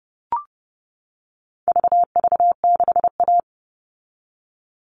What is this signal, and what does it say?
Morse code, heard as a keyed tone, about 30 wpm, 700 Hz. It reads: VV6A
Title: All Cut Numbers - 4-Digit 30wpm